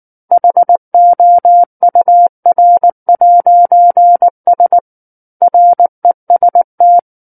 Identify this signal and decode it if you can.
Morse code, heard as a keyed tone, about 19 wpm, 700 Hz. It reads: HOUR'S REST
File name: Book One / 10-Chapter 10 - 19 WPM